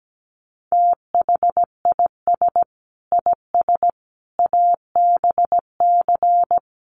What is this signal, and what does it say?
Morse code, heard as a keyed tone, about 17 wpm, 700 Hz. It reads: THIS IS ABC
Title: Sets of 5 Words 17wpm